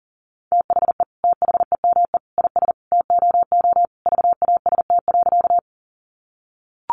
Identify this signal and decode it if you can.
Morse code, heard as a keyed tone, about 40 wpm, 700 Hz. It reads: THE THEME IS TOO VAST.
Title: Sentences from Top 1300 Words 40wpm